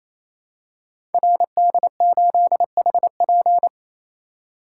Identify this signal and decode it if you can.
Morse code, heard as a keyed tone, about 27 wpm, 700 Hz. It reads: RD8HP
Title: Code Groups - 5 Letter:Numbers - 28wpm